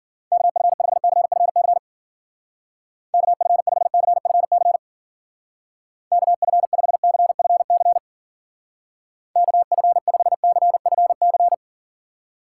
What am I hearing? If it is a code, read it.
Morse code, heard as a keyed tone, about 48 wpm, 700 Hz. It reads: KF5CFC KF5CFC KF5CFC KF5CFC
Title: Speed Racing - USA 2x3 Call Signs 40wpm